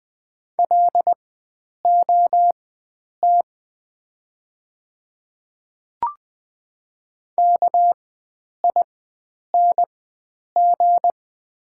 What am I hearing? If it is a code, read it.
Morse code, heard as a keyed tone, about 20 wpm, spaced as for 9 wpm, 700 Hz. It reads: LOT KING
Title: 4x Character Spacing - Top 500 Words - 20wpm